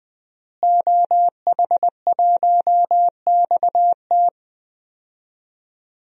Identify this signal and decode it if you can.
Morse code, heard as a keyed tone, about 20 wpm, 700 Hz. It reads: OH1XT
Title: DXCC-100 Call Signs - 5 Characters - Repeated 3x - 20wpm